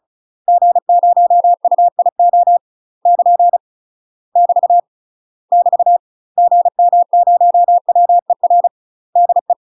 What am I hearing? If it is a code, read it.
Morse code, heard as a keyed tone, about 35 wpm, 700 Hz. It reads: G0UIO <KN> <BT> <BT> GM0WER DE
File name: UK MASTER QSOs TRAINING / 35WPM / UK MASTER QSOs 0041